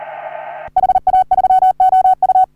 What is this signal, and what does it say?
Morse code, heard as a keyed tone, about 39 wpm, 750 Hz. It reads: HA3OU
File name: CW / HA3OU RHOBCW REC